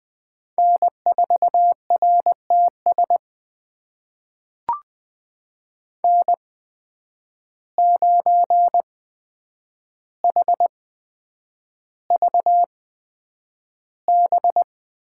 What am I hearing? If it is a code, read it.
Morse code, heard as a keyed tone, about 20 wpm, spaced as for 9 wpm, 700 Hz. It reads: N4RTS N9HVB
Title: USA Call Signs - 5 Characters - 8x Character Spacing - 20wpm